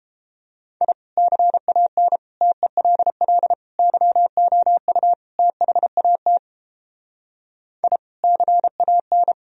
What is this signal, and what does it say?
Morse code, heard as a keyed tone, about 31 wpm, 700 Hz. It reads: I CAN TELL YOU THAT I CAN